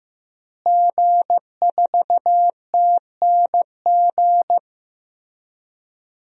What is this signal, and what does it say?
Morse code, heard as a keyed tone, about 15 wpm, 700 Hz. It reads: G4TNG
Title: DXCC-100 Call Signs - 5 Characters - Repeated 2x - 15wpm